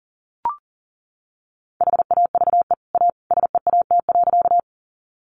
Morse code, heard as a keyed tone, about 37 wpm, 700 Hz. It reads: HAVE A SEAT.
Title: Sentences from Top 1500 Words 40wpm